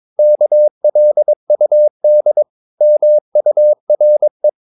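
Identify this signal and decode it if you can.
Morse code, heard as a keyed tone, about 22 wpm, 600 Hz. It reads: KLUD MURE
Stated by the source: Morse practice of short Danish words at 22 wpm, from CW med Gnister.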